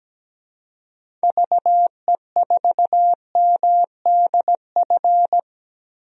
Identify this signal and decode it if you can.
Morse code, heard as a keyed tone, about 17 wpm, 700 Hz. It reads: VE4MDF
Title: DXCC-100 Call Signs - 6 Characters - 17wpm